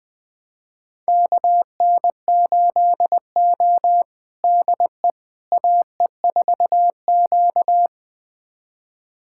Morse code, heard as a keyed tone, about 21 wpm, 700 Hz. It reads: KN8O DE AE4Q
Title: USA Callsign DE Callsign - 4 Characters - 20wpm